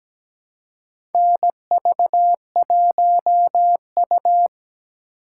Morse code, heard as a keyed tone, about 17 wpm, 700 Hz. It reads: NV1U
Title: K1USN SST QSO Practice - 17wpm